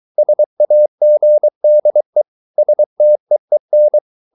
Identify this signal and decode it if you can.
Morse code, heard as a keyed tone, about 23 wpm, 600 Hz. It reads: SAGDE STEEN
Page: Ferie 23 wpm